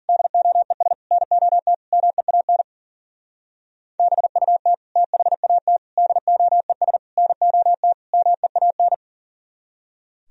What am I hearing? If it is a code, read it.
Morse code, heard as a keyed tone, about 50 wpm, 700 Hz. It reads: DOES NOT MEAN BUT THAT DOES NOT MEAN